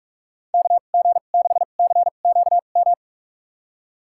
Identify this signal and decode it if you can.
Morse code, heard as a keyed tone, about 47 wpm, spaced as for 33 wpm, 700 Hz. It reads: KG6CQM
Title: USA Call Signs - 6 Characters - 2x Character Spacing - 45wpm